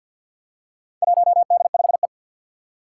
Morse code, heard as a keyed tone, about 46 wpm, 700 Hz. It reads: 1D5E
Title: Four Challenging Letter-Number Combinations 50wpm